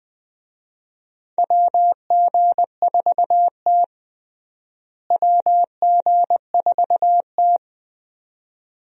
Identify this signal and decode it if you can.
Morse code, heard as a keyed tone, about 20 wpm, 700 Hz. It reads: WG4T WG4T
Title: USA Call Signs - 4 Characters - Repeated 2x - 20wpm